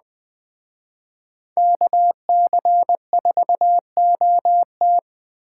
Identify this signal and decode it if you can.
Morse code, heard as a keyed tone, about 20 wpm, 700 Hz. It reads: KC4OT
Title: Speed Racing - USA 2x2 Call Signs 17wpm